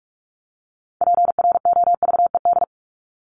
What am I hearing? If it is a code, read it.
Morse code, heard as a keyed tone, about 44 wpm, 700 Hz. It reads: PROVED
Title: ED Suffix - Top 100 Words with Base Words 45wpm